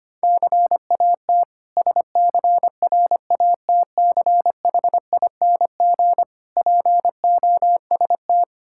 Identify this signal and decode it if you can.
Morse code, heard as a keyed tone, about 25 wpm, 700 Hz. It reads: CAT SCRATCHING POST
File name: practice / 25wpm / Difficult Words with Hints - 25wpm